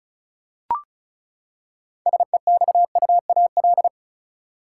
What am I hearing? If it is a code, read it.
Morse code, heard as a keyed tone, about 36 wpm, 700 Hz. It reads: SEXUAL
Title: Six Letter Words - 35wpm